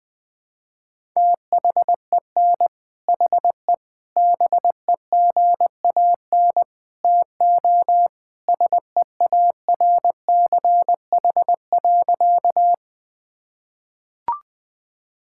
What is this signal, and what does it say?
Morse code, heard as a keyed tone, about 21 wpm, 700 Hz. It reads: THEN HE BEGAN TO SEARCH.
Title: Sentences from Top 1000 Words - Review Part 5 - 20wpm